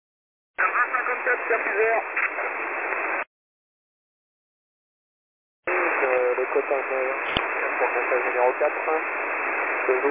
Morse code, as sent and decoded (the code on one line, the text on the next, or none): none